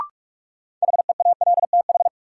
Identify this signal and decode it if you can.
Morse code, heard as a keyed tone, about 45 wpm, 700 Hz. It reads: HEALTH
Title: Sets of 5 Words - New Word Drill - 45wpm